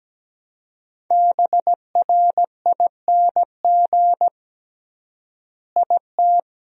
Morse code, heard as a keyed tone, about 17 wpm, 700 Hz. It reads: BRING IT
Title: 3x Word-Spacing - Sentences from Top 500 Words - 17wpm